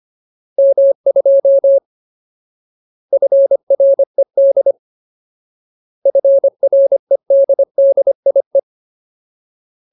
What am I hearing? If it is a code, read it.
Morse code, heard as a keyed tone, about 25 wpm, 550 Hz. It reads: M2 FRED FREDDIE
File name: Ultimate Word List / 25WPM Ultimate